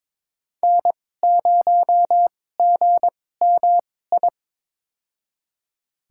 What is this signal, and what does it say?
Morse code, heard as a keyed tone, about 22 wpm, spaced as for 16 wpm, 700 Hz. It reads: N0GMI
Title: USA Call Signs - 2x Character Spacing - 22wpm